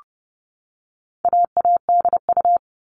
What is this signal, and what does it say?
Morse code, heard as a keyed tone, about 30 wpm, 700 Hz. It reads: AADU
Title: All Cut Numbers - 4-Digit 30wpm